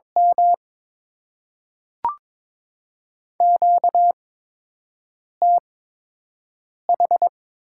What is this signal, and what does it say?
Morse code, heard as a keyed tone, about 23 wpm, 700 Hz. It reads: M Q T H
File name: Top 20 QSO Elements - 8x Character Spacing - 22wpm